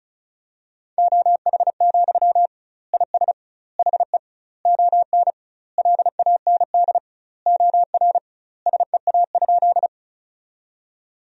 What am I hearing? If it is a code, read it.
Morse code, heard as a keyed tone, about 34 wpm, 700 Hz. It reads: OH, IS HE ON LAND OR SEA?
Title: 2x Word-Spacing - Sentences from Top 500 Words - 35wpm